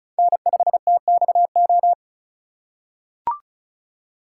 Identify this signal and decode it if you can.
Morse code, heard as a keyed tone, about 36 wpm, 700 Hz. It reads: N5TXO
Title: USA Call Signs - 5 Characters - 4x Character Spacing - 35wpm